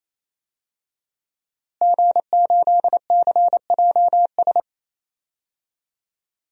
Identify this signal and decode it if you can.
Morse code, heard as a keyed tone, about 28 wpm, 700 Hz. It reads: G8CJS
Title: DXCC-100 Call Signs - 5 Characters - Repeated 3x - 28wpm